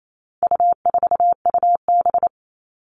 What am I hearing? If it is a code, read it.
Morse code, heard as a keyed tone, about 28 wpm, 700 Hz. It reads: U4UB
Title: All Cut Numbers - 4-Digit 28wpm